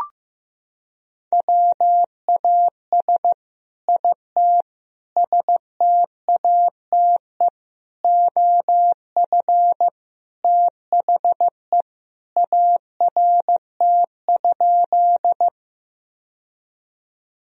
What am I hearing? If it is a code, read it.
Morse code, heard as a keyed tone, about 15 wpm, 700 Hz. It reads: WAS IT STATE OF THE ART?
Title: Sentences from Top 1000 Words - Review Part 3 - 15wpm